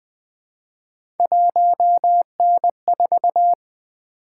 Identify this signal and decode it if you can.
Morse code, heard as a keyed tone, about 20 wpm, 700 Hz. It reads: 1N4
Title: Three Letter-Number Combinations - 20wpm